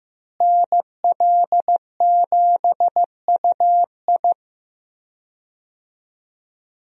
Morse code, heard as a keyed tone, about 15 wpm, 700 Hz. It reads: NL7UI
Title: USA Call Signs - 5 Characters - Repeated 2x - 15wpm